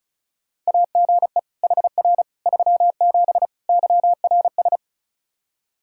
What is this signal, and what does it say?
Morse code, heard as a keyed tone, about 34 wpm, 700 Hz. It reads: AGE HR 37 YRS
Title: Practice QSO - Exchange 3 - More Info - 35wpm